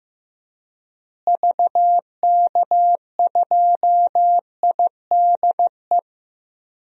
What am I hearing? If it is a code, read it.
Morse code, heard as a keyed tone, about 15 wpm, 700 Hz. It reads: VK2IDE
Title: DXCC-100 Call Signs - 6 Characters - Repeated 2x - 15wpm